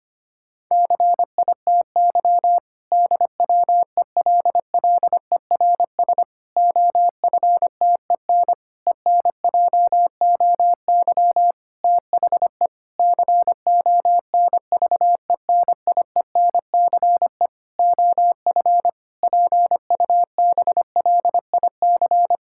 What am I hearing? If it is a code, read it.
Morse code, heard as a keyed tone, about 25 wpm, 700 Hz. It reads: CITY DWELLERS OFTEN ENJOY THE CONVENIENCE OF PUBLIC
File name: practice / 25wpm / Difficult Sentences with Hints - 25wpm